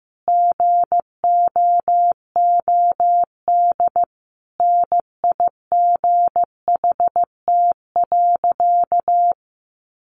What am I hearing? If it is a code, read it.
Morse code, heard as a keyed tone, about 15 wpm, 700 Hz. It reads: GOOD NIGHT.
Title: Sentences from Top 300 Words 15wpm